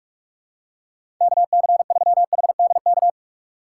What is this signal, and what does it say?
Morse code, heard as a keyed tone, about 46 wpm, 700 Hz. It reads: KC3HDK
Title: USA Call Signs - 6 Characters - 3x Character Spacing - 45wpm